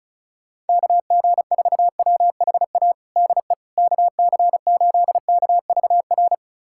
Morse code, heard as a keyed tone, about 36 wpm, 700 Hz. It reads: KG4WHA DE KC8KVR
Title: USA Callsign DE Callsign - 6 Characters - 35wpm